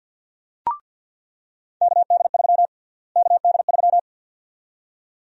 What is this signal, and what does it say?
Morse code, heard as a keyed tone, about 48 wpm, 700 Hz. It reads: KD3 KD3
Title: USA Call Sign Prefixes - Repeated 2x - 50wpm